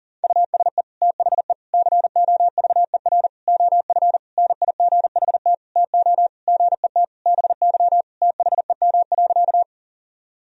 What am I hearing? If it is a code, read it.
Morse code, heard as a keyed tone, about 40 wpm, 700 Hz. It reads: USE THE COVER OF NIGHT TO GET BY THEM.
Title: Sentences from Top 1000 Words - Review Part 1 - 40wpm